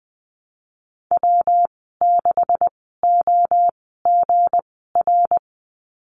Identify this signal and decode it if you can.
Morse code, heard as a keyed tone, about 20 wpm, spaced as for 14 wpm, 700 Hz. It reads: W6OGR
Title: 238 Lesson 38g - Letter q - Call Signs 20wpm